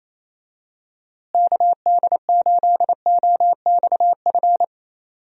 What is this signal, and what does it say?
Morse code, heard as a keyed tone, about 28 wpm, 700 Hz. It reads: KD8OXF
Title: USA Call Signs - 6 Characters - Repeated 3x - 28wpm